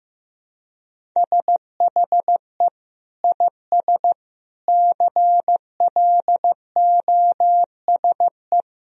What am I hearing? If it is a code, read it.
Morse code, heard as a keyed tone, about 15 wpm, 700 Hz. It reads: SHE IS CLOSE
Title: Sentences from Top 1000 Words - Review Part 1 - 15wpm